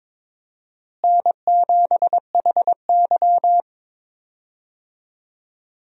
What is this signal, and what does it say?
Morse code, heard as a keyed tone, about 22 wpm, 700 Hz. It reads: N7HY
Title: USA Call Signs - 4 Characters - Repeated 2x - 22wpm